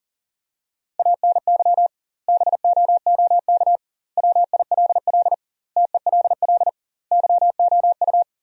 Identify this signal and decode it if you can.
Morse code, heard as a keyed tone, about 40 wpm, 700 Hz. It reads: ANY BOOK WILL TELL YOU
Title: 2x Word-Spacing - Sentences from Top 300 Words - 40wpm